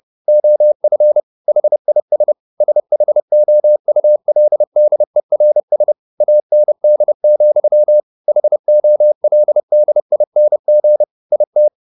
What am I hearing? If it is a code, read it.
Morse code, heard as a keyed tone, about 30 wpm, 600 Hz. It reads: OF HIS SHOULDERS AND, HOLDING IT
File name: the old man and the sea / the old man and the sea 1 30wpm